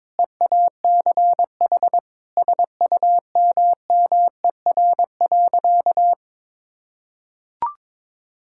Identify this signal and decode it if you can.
Morse code, heard as a keyed tone, about 21 wpm, 700 Hz. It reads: EACH SUMMER.
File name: Difficult Sentences with Hints - 22wpm